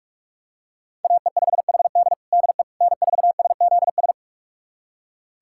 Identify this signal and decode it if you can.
Morse code, heard as a keyed tone, about 46 wpm, 700 Hz. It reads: AE5HD DE N4SZS
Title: USA Callsign DE Callsign - 5 Characters - 45wpm